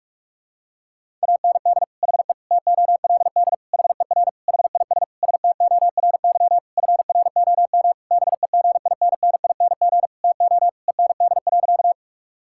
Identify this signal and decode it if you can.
Morse code, heard as a keyed tone, about 41 wpm, 690 Hz. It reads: AND HE TOLD HER HIS STORY FROM BEGINNING TO END.
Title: Sentences from Top 1000 Words - Review Part 5 - 45wpm